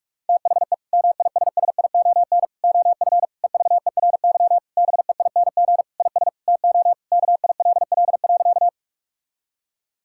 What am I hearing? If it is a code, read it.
Morse code, heard as a keyed tone, about 46 wpm, 700 Hz. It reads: THE MISSION OF EVERY BEING IS TO KILL.
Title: Sentences from Top 1600 Words 45wpm